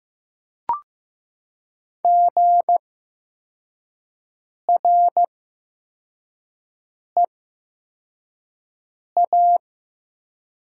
Morse code, heard as a keyed tone, about 15 wpm, spaced as for 4 wpm, 700 Hz. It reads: GREA
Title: 8x Character Spacing - Top 200 Words - 15wpm